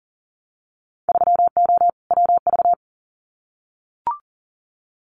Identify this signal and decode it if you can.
Morse code, heard as a keyed tone, about 37 wpm, 700 Hz. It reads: 3O WV
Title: Winter Field Day Exchanges 40wpm